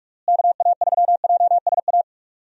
Morse code, heard as a keyed tone, about 41 wpm, 700 Hz. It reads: KA3JSA
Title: USA Call Signs - 6 Characters - Repeated 3x - 45wpm